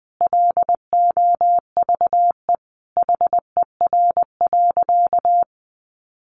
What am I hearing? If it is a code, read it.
Morse code, heard as a keyed tone, about 19 wpm, 700 Hz. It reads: LOVE HER.
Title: Sentences from Top 400 Words 20wpm